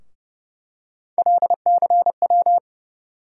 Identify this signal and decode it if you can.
Morse code, heard as a keyed tone, about 32 wpm, 700 Hz. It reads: LCW
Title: USA Call Sign Postfixes - 3 Letters 30wpm